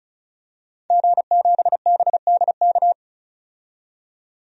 Morse code, heard as a keyed tone, about 36 wpm, 700 Hz. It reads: G7BDK
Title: DXCC-100 Call Signs - 5 Characters - 35wpm